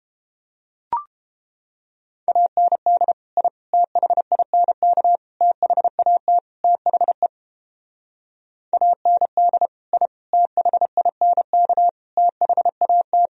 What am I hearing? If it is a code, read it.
Morse code, heard as a keyed tone, about 29 wpm, 700 Hz. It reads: AND I THINK THAT THE AND I THINK THAT